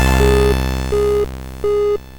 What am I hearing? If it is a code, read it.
There is no code.